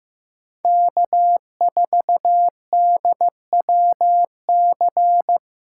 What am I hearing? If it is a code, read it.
Morse code, heard as a keyed tone, about 15 wpm, 700 Hz. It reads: K4DWC
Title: USA Call Signs - 6x Character Spacing - 15wpm